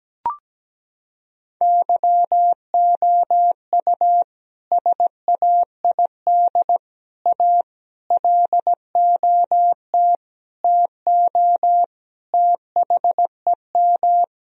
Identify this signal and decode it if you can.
Morse code, heard as a keyed tone, about 17 wpm, 700 Hz. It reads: YOU SAID A LOT TO THEM
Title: Sentences from Top 1000 Words - Review Part 3 - 17wpm